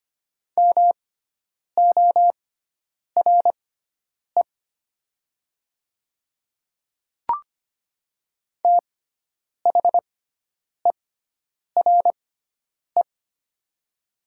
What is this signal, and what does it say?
Morse code, heard as a keyed tone, about 25 wpm, spaced as for 9 wpm, 700 Hz. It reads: MORE THERE
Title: 6x Character Spacing - Top 100 Words - 25wpm